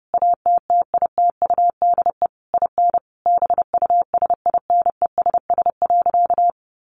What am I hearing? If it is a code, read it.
Morse code, heard as a keyed tone, about 30 wpm, 700 Hz. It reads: ATTITUDE IN BUSINESS.
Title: Sentences from Top 1300 Words 30wpm